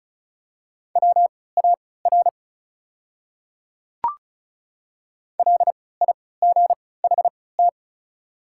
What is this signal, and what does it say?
Morse code, heard as a keyed tone, about 36 wpm, spaced as for 20 wpm, 700 Hz. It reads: WAR LIGHT
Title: 3x Character Spacing - Top 500 Words - 35wpm